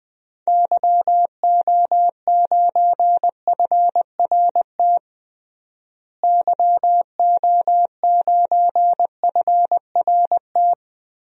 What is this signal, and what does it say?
Morse code, heard as a keyed tone, about 21 wpm, 700 Hz. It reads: YO9FRT YO9FRT
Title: DXCC-100 Call Signs - 6 Characters - Repeated 3x - 20wpm